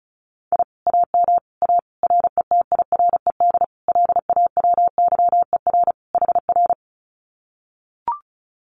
Morse code, heard as a keyed tone, about 34 wpm, 700 Hz. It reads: I AM A RETIRED LAWYER HR
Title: Common QSO Phrases Encore 35wpm